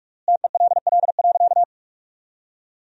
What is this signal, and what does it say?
Morse code, heard as a keyed tone, about 46 wpm, 700 Hz. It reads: TELL.
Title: 3x Word-Spacing - Sentences from Top 300 Words - 45wpm